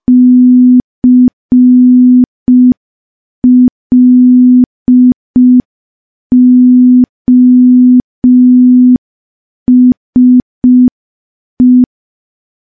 Morse code, heard as a keyed tone, about 5 wpm, 250 Hz. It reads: CLOSE
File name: morse / morse-training-05WPM